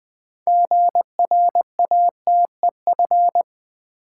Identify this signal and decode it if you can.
Morse code, heard as a keyed tone, about 20 wpm, 700 Hz. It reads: GRATEF
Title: Build A Word - 2 Letters At A Time - 8-Letter Words - 20wpm